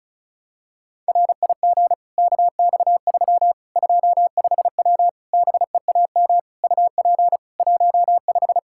Morse code, heard as a keyed tone, about 36 wpm, 700 Hz. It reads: RIG KX3 25W BEAM UP 15
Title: K1USN SST QSO Practice 35wpm